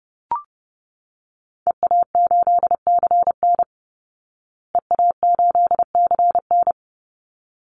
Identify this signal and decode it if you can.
Morse code, heard as a keyed tone, about 30 wpm, 700 Hz. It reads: EA8CN EA8CN
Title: Call Signs Repeated 3x - 30wpm